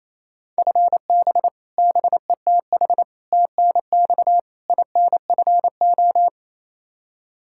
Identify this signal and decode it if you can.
Morse code, heard as a keyed tone, about 28 wpm, 700 Hz. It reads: FB BETH TNX INFO
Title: Practice QSO - Exchange 3 - More Info - 28wpm